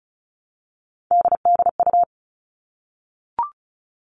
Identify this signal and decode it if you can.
Morse code, heard as a keyed tone, about 34 wpm, 700 Hz. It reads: DDU
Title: All Cut Numbers - 3-Digit 35wpm